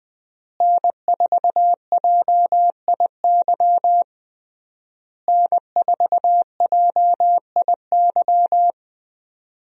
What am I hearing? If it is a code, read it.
Morse code, heard as a keyed tone, about 21 wpm, 700 Hz. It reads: N4JIY N4JIY
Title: USA Call Signs - 5 Characters - Repeated 3x - 20wpm